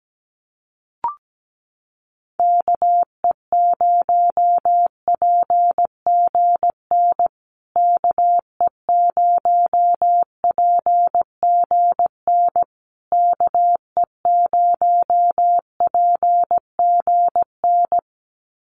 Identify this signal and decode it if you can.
Morse code, heard as a keyed tone, about 17 wpm, 700 Hz. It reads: KE0PGN KE0PGN KE0PGN
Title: USA Call Signs Repeated 3x 17wpm